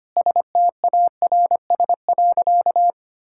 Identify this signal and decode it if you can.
Morse code, heard as a keyed tone, about 25 wpm, 700 Hz. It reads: STARS.
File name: practice / 25wpm / Sentences with 3-Word Phrases - 25wpm